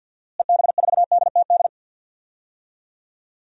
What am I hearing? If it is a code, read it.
Morse code, heard as a keyed tone, about 49 wpm, 700 Hz. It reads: EB4DTD
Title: DXCC-100 Call Signs - 6 Characters - Repeated 3x - 50wpm